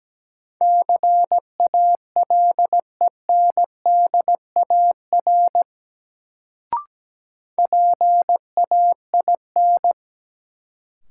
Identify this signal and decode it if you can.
Morse code, heard as a keyed tone, about 17 wpm, 700 Hz. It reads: CALENDAR PAIN
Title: Sentences from Top 1600 Words - New Word Drill - 17wpm